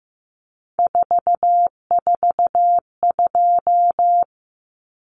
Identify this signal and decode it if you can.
Morse code, heard as a keyed tone, about 15 wpm, 700 Hz. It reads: 442
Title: Numbers - 3-Digit 15wpm